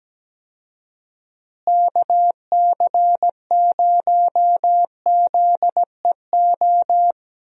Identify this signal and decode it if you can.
Morse code, heard as a keyed tone, about 17 wpm, 700 Hz. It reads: KC0ZEO
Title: USA Call Signs - 6 Characters - Repeated 2x - 17wpm